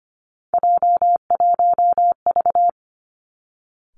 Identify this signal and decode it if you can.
Morse code, heard as a keyed tone, about 25 wpm, 700 Hz. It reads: J1V
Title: Three Challenging Letter-Number Combinations 25wpm